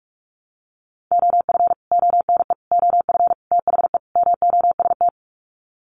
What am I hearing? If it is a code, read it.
Morse code, heard as a keyed tone, about 44 wpm, 700 Hz. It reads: OF ONE OF THE MOST
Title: Sets of 5 Words Encore2 45wpm